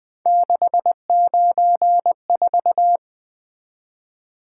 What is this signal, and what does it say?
Morse code, heard as a keyed tone, about 20 wpm, 700 Hz. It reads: 694
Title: Three Letter-Number Combinations - 20wpm